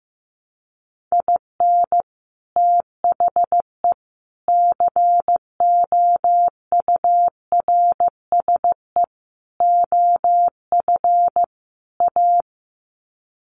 Morse code, heard as a keyed tone, about 15 wpm, 700 Hz. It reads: IN THE COURSE OF A
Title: Sets of 5 Words Encore2 15wpm